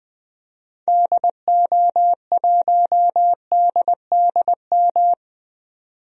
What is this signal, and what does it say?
Morse code, heard as a keyed tone, about 20 wpm, 700 Hz. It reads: DO1DDM
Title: DXCC-100 Call Signs - 6 Characters - 20wpm